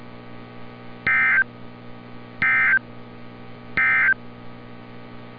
Emergency Alert System (SAME) tones, heard as harsh buzzing bursts. It is heard from 0.9 to 4.2 s.
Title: NOAA Weather Radio - Altoona, PA